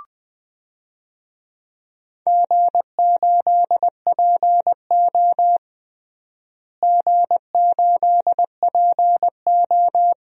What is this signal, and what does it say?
Morse code, heard as a keyed tone, about 20 wpm, 700 Hz. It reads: G8PO G8PO
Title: DXCC-100 Call Signs - 4 Characters - Repeated 3x - 20wpm